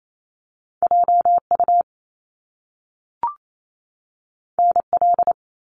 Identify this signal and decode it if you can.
Morse code, heard as a keyed tone, about 28 wpm, 700 Hz. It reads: JU NL
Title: Top Two Letter Combinations in Words 28wpm